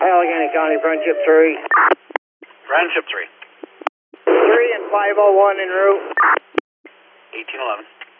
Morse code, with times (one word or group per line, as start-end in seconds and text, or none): none